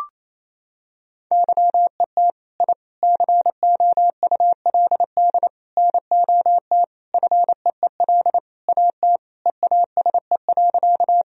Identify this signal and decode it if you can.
Morse code, heard as a keyed tone, about 28 wpm, 700 Hz. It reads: YET I COULD NOT FEEL AT EASE.
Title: Sentences from Top 1000 Words - Review Part 1 - 28wpm